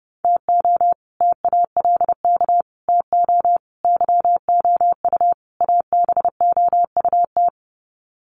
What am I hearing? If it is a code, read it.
Morse code, heard as a keyed tone, about 30 wpm, 700 Hz. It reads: TO TALK TO YOU ABOUT
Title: Sets of 5 Words Encore 30wpm